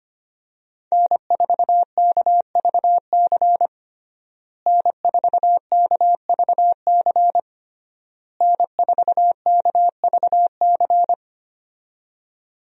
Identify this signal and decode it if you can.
Morse code, heard as a keyed tone, about 25 wpm, 700 Hz. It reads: N4KVC N4KVC N4KVC
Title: USA Call Signs - 5 Characters - Repeated 3x - 25wpm